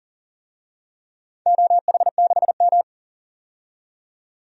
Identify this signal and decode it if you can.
Morse code, heard as a keyed tone, about 40 wpm, 700 Hz. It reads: OH6M
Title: DXCC-100 Call Signs - 4 Characters - 40wpm